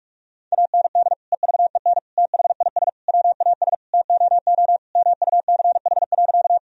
Morse code, heard as a keyed tone, about 46 wpm, 700 Hz. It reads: AND EVEN THIS WAS TOO MUCH.
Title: Sentences from Top 1000 Words - Review Part 1 - 45wpm